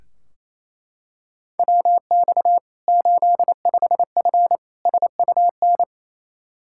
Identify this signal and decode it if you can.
Morse code, heard as a keyed tone, about 28 wpm, 700 Hz. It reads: WX 85F SUN
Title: Practice QSO - Exchange 2 - Info - 28wpm